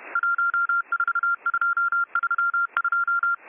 Morse code, heard as a keyed tone, about 31 wpm, 1390 Hz. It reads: 14232